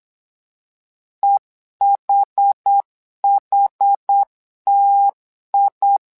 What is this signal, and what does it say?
Morse code, heard as a keyed tone, about 8 wpm, 800 Hz. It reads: EHHTI
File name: LEZIONI / 2 UHDN / 2 UH 8wpm 2129